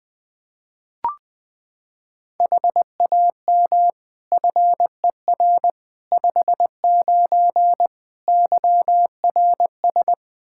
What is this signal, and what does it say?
Morse code, heard as a keyed tone, about 21 wpm, 700 Hz. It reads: HAM FER 59 YRS
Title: Practice QSO - Exchange 3 - More Info - 20wpm